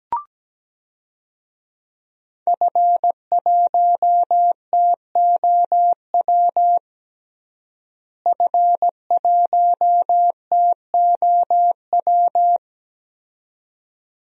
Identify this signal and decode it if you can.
Morse code, heard as a keyed tone, about 17 wpm, 700 Hz. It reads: F1TOW F1TOW
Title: DXCC-100 Call Signs - 5 Characters - Repeated 2x - 17wpm